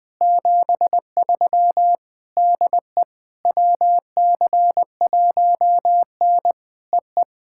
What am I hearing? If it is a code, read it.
Morse code, heard as a keyed tone, about 20 wpm, 700 Hz. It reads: 73 DE WC1N EE
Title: POTA QSO Practice - 20wpm